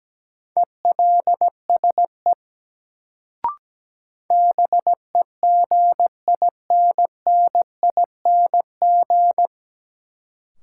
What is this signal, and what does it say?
Morse code, heard as a keyed tone, about 17 wpm, 700 Hz. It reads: ELSE BEGINNING
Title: Sentences from Top 800 Words - New Word Drill - 17wpm